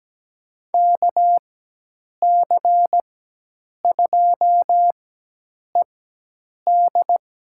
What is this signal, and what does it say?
Morse code, heard as a keyed tone, about 17 wpm, spaced as for 8 wpm, 700 Hz. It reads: KC2ED
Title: USA Call Signs - 4x Character Spacing - 17wpm